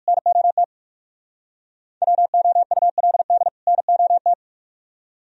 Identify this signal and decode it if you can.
Morse code, heard as a keyed tone, about 58 wpm, 690 Hz. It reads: NOT WOULD NOT